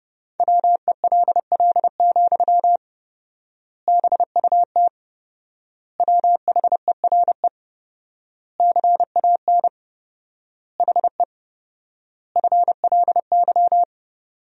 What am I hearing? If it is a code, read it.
Morse code, heard as a keyed tone, about 31 wpm, 700 Hz. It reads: WELL, BUT WHERE CAN HE FLY
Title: 4x Word-Spacing - Sentences from Top 500 Words - 30wpm